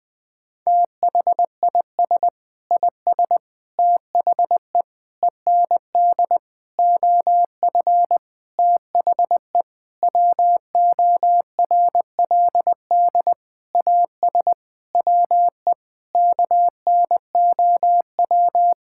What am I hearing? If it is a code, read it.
Morse code, heard as a keyed tone, about 20 wpm, 700 Hz. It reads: THIS IS THE END OF THE WORLD AS WE KNOW
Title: Sentences with 9-Word Phrases - 20wpm